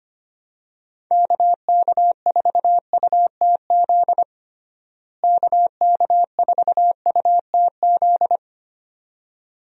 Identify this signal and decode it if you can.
Morse code, heard as a keyed tone, about 25 wpm, 700 Hz. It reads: KK4UTZ KK4UTZ
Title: USA Call Signs - 6 Characters - Repeated 2x - 25wpm